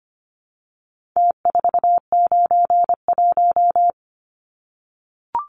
Morse code, heard as a keyed tone, about 25 wpm, 700 Hz. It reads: T491
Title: Numbers - Contest Style - 4-Digit 25wpm